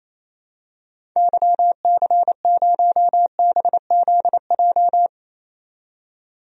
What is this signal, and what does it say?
Morse code, heard as a keyed tone, about 28 wpm, 700 Hz. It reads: YC0BZJ
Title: DXCC-100 Call Signs - 6 Characters - 28wpm